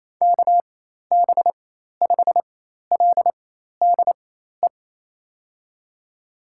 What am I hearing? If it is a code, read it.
Morse code, heard as a keyed tone, about 28 wpm, spaced as for 13 wpm, 700 Hz. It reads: KB5LDE
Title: USA Call Signs - 4x Character Spacing - 28wpm